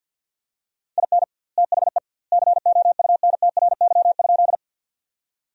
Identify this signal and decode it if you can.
Morse code, heard as a keyed tone, about 53 wpm, 700 Hz. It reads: IN THE COUNTRY?
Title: 2x Word-Spacing - Sentences from Top 300 Words - 50wpm